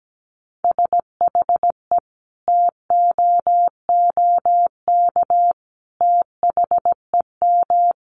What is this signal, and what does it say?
Morse code, heard as a keyed tone, about 17 wpm, 700 Hz. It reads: SHE TOOK THEM
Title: Sentences from Top 1300 Words 17wpm